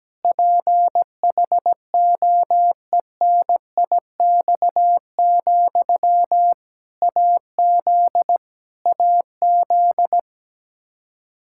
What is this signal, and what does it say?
Morse code, heard as a keyed tone, about 17 wpm, 700 Hz. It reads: PHOENIX, AZ AZ
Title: K1USN SST QSO Practice 17wpm